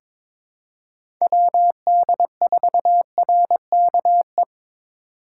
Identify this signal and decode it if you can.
Morse code, heard as a keyed tone, about 23 wpm, 700 Hz. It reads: WD4RKE
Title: USA Call Signs - 3x Character Spacing - 22wpm